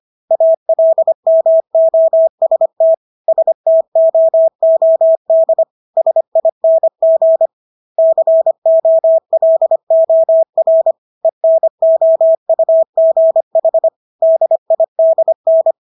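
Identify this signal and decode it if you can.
Morse code, heard as a keyed tone, about 25 wpm, 650 Hz. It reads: ALMOST STOOD SING COLOR ENOUGH DIDN
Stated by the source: Morse practice of English words at 25 wpm from CW med Gnister.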